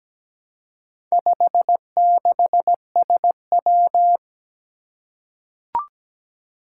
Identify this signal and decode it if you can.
Morse code, heard as a keyed tone, about 17 wpm, 700 Hz. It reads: 56SW
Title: Four Challenging Letter-Number Combinations 17wpm